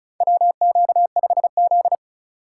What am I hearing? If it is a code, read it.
Morse code, heard as a keyed tone, about 36 wpm, 700 Hz. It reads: WQ5Z
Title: USA Call Signs - 4 Characters - 6x Character Spacing - 35wpm